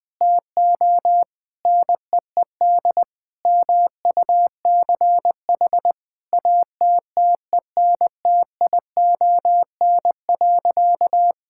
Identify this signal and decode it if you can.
Morse code, heard as a keyed tone, about 20 wpm, 700 Hz. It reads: TO NEED MUCH ATTENTION.
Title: Sentences from Top 1000 Words - Review Part 6 - 20wpm